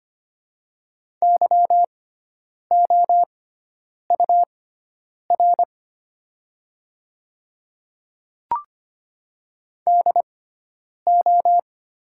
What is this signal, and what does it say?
Morse code, heard as a keyed tone, about 25 wpm, spaced as for 9 wpm, 700 Hz. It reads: YOUR DO
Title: 6x Character Spacing - Top 300 Words - 25wpm